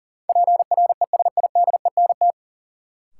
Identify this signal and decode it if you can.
Morse code, heard as a keyed tone, about 40 wpm, 700 Hz. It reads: PRESIDENT
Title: Sentences from Top 1000 Words - New Word Drill - 40wpm